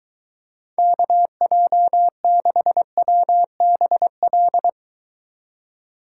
Morse code, heard as a keyed tone, about 23 wpm, 700 Hz. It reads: KJ6WBL